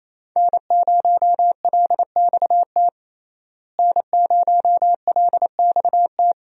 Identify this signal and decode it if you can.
Morse code, heard as a keyed tone, about 28 wpm, 700 Hz. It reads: N0LXT N0LXT
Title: USA Call Signs - 5 Characters - Repeated 3x - 28wpm